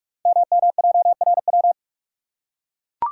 Morse code, heard as a keyed tone, about 44 wpm, 700 Hz. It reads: MMJRW
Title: Code Groups - 5 Letters - 45wpm